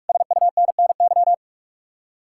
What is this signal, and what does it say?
Morse code, heard as a keyed tone, about 59 wpm, spaced as for 52 wpm, 700 Hz. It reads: SUNNY
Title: Speed-Racing Common QSO Phrases - 45wpm